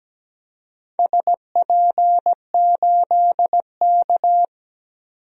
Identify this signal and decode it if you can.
Morse code, heard as a keyed tone, about 17 wpm, 700 Hz. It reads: SP8K
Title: DXCC-100 Call Signs - 4 Characters - 17wpm